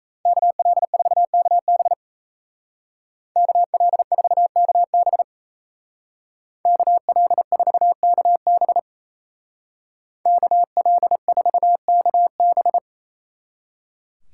Morse code, heard as a keyed tone, about 36 wpm, 700 Hz. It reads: KL4KB KL4KB KL4KB KL4KB